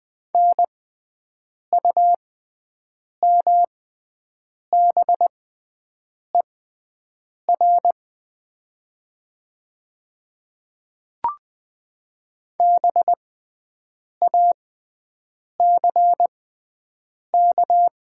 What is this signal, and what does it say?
Morse code, heard as a keyed tone, about 20 wpm, spaced as for 7 wpm, 700 Hz. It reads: NUMBER BACK
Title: 6x Character Spacing - Top 200 Words - 20wpm